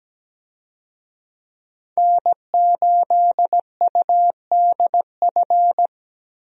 Morse code, heard as a keyed tone, about 17 wpm, 700 Hz. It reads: N8UDF
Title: USA Call Signs - 5 Characters - Repeated 2x - 17wpm